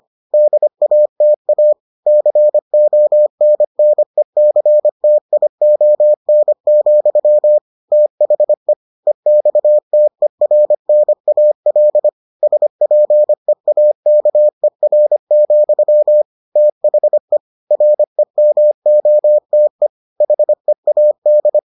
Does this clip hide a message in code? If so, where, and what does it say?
Yes: Morse code from 0.3 to 21.6 s, heard as a keyed tone, about 25 wpm, 600 Hz. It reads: DATA CONNECTION, THE EXTERNAL SPEAKER, THE REMOTE HEAD